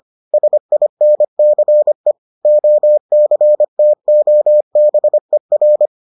Morse code, heard as a keyed tone, about 25 wpm, 600 Hz. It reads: SINCE OCTOBER